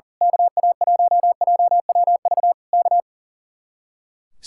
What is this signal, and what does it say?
Morse code, heard as a keyed tone, about 37 wpm, 700 Hz. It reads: KA1JWV K
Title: Speed-Racing Common QSO Phrases - 40wpm